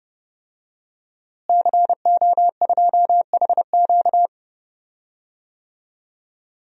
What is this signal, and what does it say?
Morse code, heard as a keyed tone, about 30 wpm, 700 Hz. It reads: CO2HQ
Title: DXCC-100 Call Signs - 5 Characters - Repeated 2x - 30wpm